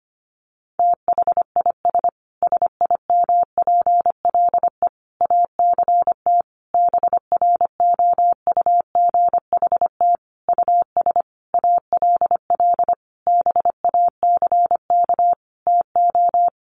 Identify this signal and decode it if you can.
Morse code, heard as a keyed tone, about 25 wpm, 700 Hz. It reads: THIS SIMPLE ACT BROUGHT US ALL BACK TO
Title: Sentences from Top 1500 Words 25wpm